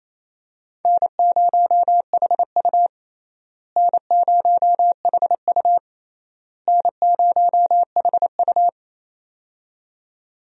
Morse code, heard as a keyed tone, about 27 wpm, 700 Hz. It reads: N0HU N0HU N0HU
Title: USA Call Signs - 4 Characters - Repeated 3x - 28wpm